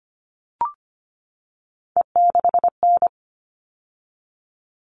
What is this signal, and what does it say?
Morse code, heard as a keyed tone, about 25 wpm, 700 Hz. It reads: E6N
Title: All Cut Numbers - 3-Digit 25wpm